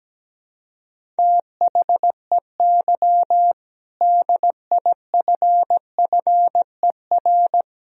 Morse code, heard as a keyed tone, about 17 wpm, 700 Hz. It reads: THEY DIFFER
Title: Sentences from Top 1000 Words - Review Part 1 - 17wpm